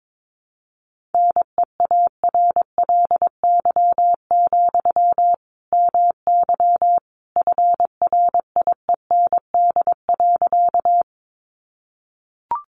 Morse code, heard as a keyed tone, about 22 wpm, 700 Hz. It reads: NEARLY, MY FRIEND.
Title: Sentences from Top 1300 Words 22wpm